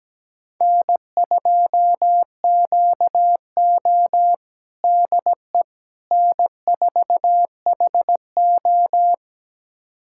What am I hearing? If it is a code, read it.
Morse code, heard as a keyed tone, about 17 wpm, 700 Hz. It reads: N2QO DE N4HO
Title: USA Callsign DE Callsign - 4 Characters - 17wpm